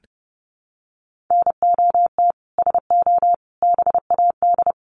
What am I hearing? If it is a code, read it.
Morse code, heard as a keyed tone, about 30 wpm, 700 Hz. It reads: NOT SO BAD
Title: Sentences from Top 700 Words 30wpm